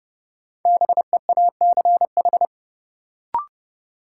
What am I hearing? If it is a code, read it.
Morse code, heard as a keyed tone, about 32 wpm, 700 Hz. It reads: BEACH
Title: Sentences from Top 1500 Words - New Word Drill Alt - 30wpm